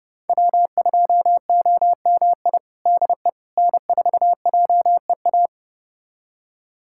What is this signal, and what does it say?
Morse code, heard as a keyed tone, about 30 wpm, 700 Hz. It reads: W2OMI DE N4JEA
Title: USA Callsign DE Callsign - 5 Characters - 30wpm